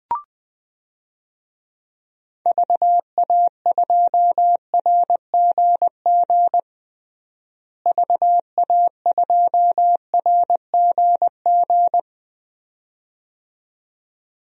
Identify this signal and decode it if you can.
Morse code, heard as a keyed tone, about 20 wpm, 700 Hz. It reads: VA2RGG VA2RGG
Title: DXCC-100 Call Signs - 6 Characters - Repeated 2x - 20wpm